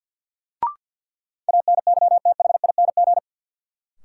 Morse code, heard as a keyed tone, about 53 wpm, 700 Hz. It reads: ANYTHING
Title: Sentences from Top 1000 Words - New Word Drill - 50wpm